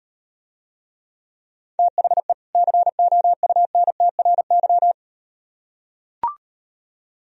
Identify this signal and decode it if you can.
Morse code, heard as a keyed tone, about 38 wpm, 700 Hz. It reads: THE COUNTRY